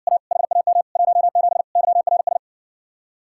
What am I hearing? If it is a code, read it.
Morse code, heard as a keyed tone, about 60 wpm, 700 Hz. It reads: A HAM 17 YRS